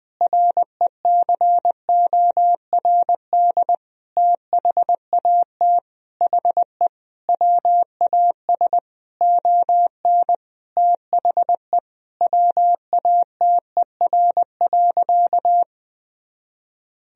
Morse code, heard as a keyed tone, about 21 wpm, 700 Hz. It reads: RECORD THAT HE WAS ON THE WATER.
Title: Sentences from Top 500 Words and Abbreviations 20wpm